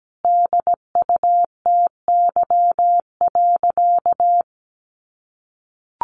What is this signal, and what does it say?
Morse code, heard as a keyed tone, about 17 wpm, 700 Hz. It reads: DUTY.
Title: Sentences from Top 1500 Words 17wpm